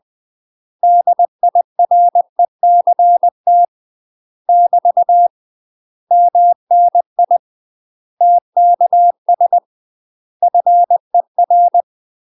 Morse code, heard as a keyed tone, about 20 wpm, 700 Hz. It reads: DIRECT <BT> MNI TKS FER